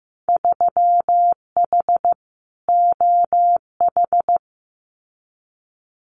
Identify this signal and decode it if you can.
Morse code, heard as a keyed tone, about 15 wpm, 700 Hz. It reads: 3H OH
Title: Winter Field Day Exchanges Encore 15wpm